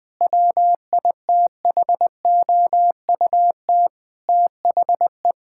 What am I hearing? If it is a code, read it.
Morse code, heard as a keyed tone, about 20 wpm, 700 Hz. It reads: WITHOUT THE
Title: Sets of 8 Words - 20wpm